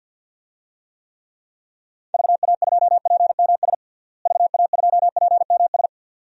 Morse code, heard as a keyed tone, about 52 wpm, 700 Hz. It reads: VA2PMS VA2PMS
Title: DXCC-100 Call Signs - 6 Characters - Repeated 2x - 50wpm